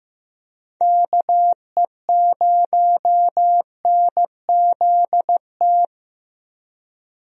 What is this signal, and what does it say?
Morse code, heard as a keyed tone, about 15 wpm, 700 Hz. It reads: KE0NZT
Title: USA Call Signs - 6 Characters - 15wpm